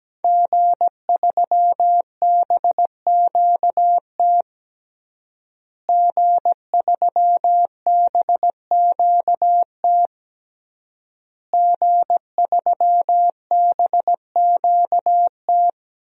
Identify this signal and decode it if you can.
Morse code, heard as a keyed tone, about 17 wpm, 700 Hz. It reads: G3BQT G3BQT G3BQT
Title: DXCC-100 Call Signs - 5 Characters - Repeated 3x - 17wpm